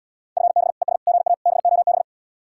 Morse code, heard as a keyed tone, about 79 wpm, 700 Hz. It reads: FB ES GA BOB